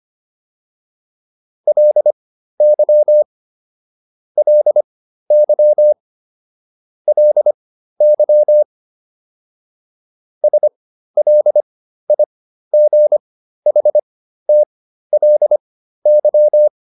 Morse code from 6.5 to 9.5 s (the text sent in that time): LY